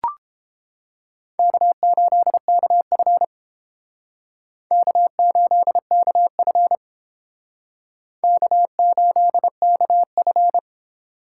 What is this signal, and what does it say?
Morse code, heard as a keyed tone, about 31 wpm, 700 Hz. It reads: K8KF K8KF K8KF